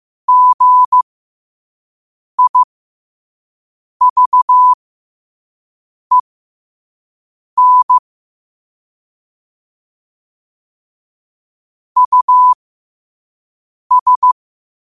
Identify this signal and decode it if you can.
Morse code, heard as a keyed tone, about 15 wpm, spaced as for 5 wpm, 1000 Hz. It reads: GIVEN US
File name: qotdfiles / 5wpm